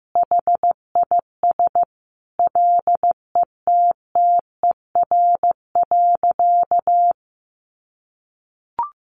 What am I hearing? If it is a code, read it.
Morse code, heard as a keyed tone, about 15 wpm, 700 Hz. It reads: HIS LETTER.
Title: Sentences from Top 1500 Words 15wpm